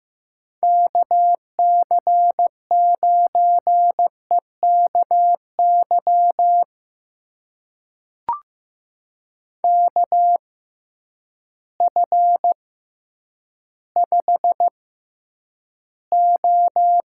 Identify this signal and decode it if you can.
Morse code, heard as a keyed tone, about 15 wpm, spaced as for 9 wpm, 700 Hz. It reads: KC9EKY KF5O
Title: USA Call Signs - 6 Characters - 6x Character Spacing - 15wpm